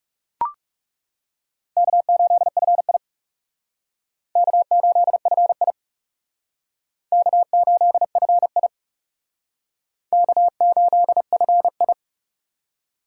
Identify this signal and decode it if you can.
Morse code, heard as a keyed tone, about 36 wpm, 700 Hz. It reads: K8FI K8FI K8FI K8FI
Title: Speed Racing - USA 1x2 Call Signs 30wpm